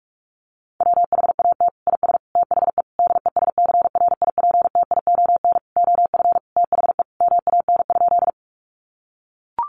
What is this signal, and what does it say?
Morse code, heard as a keyed tone, about 44 wpm, 700 Hz. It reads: WHAT IS THE DESCRIPTION OF THE MAN?
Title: Sentences from Top 1300 Words 45wpm